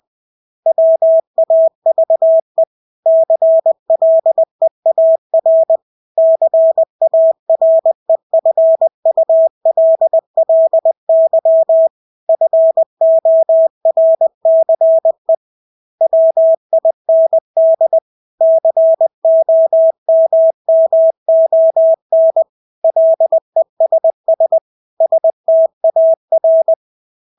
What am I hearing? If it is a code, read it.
Morse code, heard as a keyed tone, about 20 wpm, 650 Hz. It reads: WAVE CLEAR CAREFULLY FORCE WIND COMMON LESS STAR
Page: Engelske ord 20 wpm